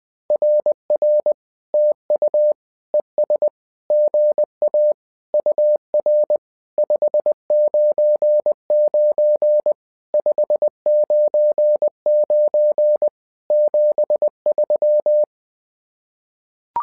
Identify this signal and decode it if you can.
Morse code, heard as a keyed tone, about 20 wpm, 600 Hz. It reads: RR TU ES GA UR 599 599 73
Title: SOTA QSO Practice 20wpm